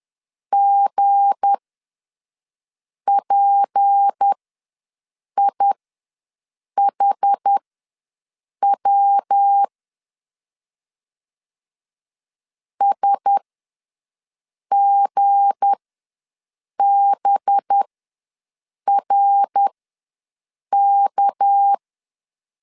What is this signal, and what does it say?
Morse code, heard as a keyed tone, about 10 wpm, spaced as for 6 wpm, 800 Hz. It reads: GPIHW SGBRK
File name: LEZIONI / 7 GF9Z / 7 G ws 10wpm 3843